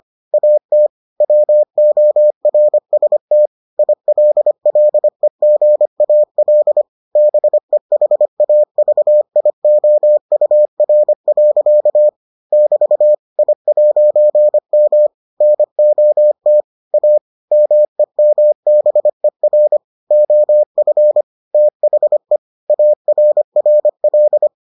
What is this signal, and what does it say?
Morse code, heard as a keyed tone, about 25 wpm, 600 Hz. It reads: AT WORST ILLEGAL BEHAVIOUR. <BT> I'M NOT A MEMBER OF THE ARRL